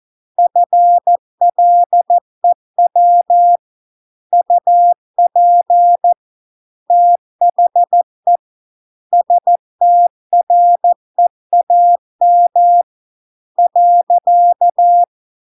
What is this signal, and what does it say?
Morse code, heard as a keyed tone, about 14 wpm, 700 Hz. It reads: FLEW UP THE STREAM .